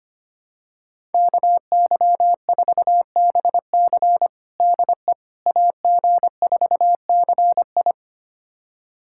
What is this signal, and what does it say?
Morse code, heard as a keyed tone, about 25 wpm, 700 Hz. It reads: KY4BC DE AG4CI
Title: USA Callsign DE Callsign - 5 Characters - 25wpm